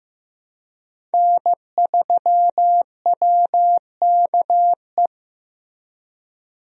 Morse code, heard as a keyed tone, about 15 wpm, 700 Hz. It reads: N3WKE
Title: USA Call Signs - 5 Characters - Repeated 2x - 15wpm